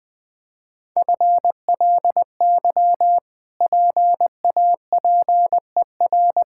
Morse code, heard as a keyed tone, about 21 wpm, 700 Hz. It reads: FLY PAPER
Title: Easy Sentences with Hints - 20wpm